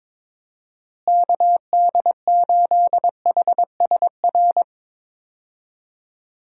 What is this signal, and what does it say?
Morse code, heard as a keyed tone, about 22 wpm, 700 Hz. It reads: KD8HSR
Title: USA Call Signs - 6 Characters - Repeated 2x - 22wpm